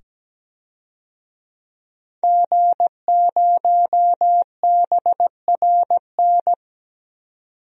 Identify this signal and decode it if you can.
Morse code, heard as a keyed tone, about 17 wpm, 700 Hz. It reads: G0BRN
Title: DXCC-100 Call Signs - 5 Characters - 17wpm